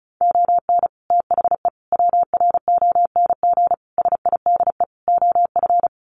Morse code, heard as a keyed tone, about 34 wpm, 700 Hz. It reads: ON THE WRONG SIDE OF
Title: Sets of 5 Words Encore 35wpm